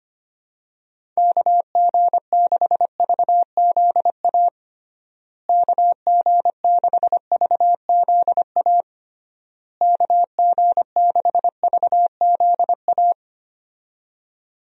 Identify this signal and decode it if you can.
Morse code, heard as a keyed tone, about 25 wpm, 700 Hz. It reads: KG6VZA KG6VZA KG6VZA
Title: USA Call Signs - 6 Characters - Repeated 3x - 25wpm